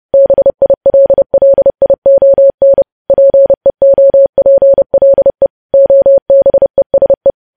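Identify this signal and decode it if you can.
Morse code, heard as a keyed tone, about 30 wpm, 550 Hz. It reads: BILLION PEOPLE OBESE
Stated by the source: Morse news headlines at 30 wpm.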